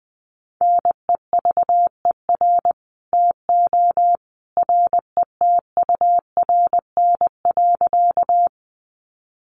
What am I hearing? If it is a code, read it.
Morse code, heard as a keyed tone, about 20 wpm, 700 Hz. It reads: NEVER TO RETURN.
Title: Sentences from Top 800 Words 20wpm